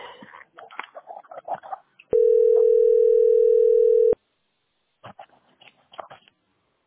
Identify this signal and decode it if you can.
Ringback tone (North America), 440 + 480 Hz.
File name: 64b966c3-ec1e-4fbd-8785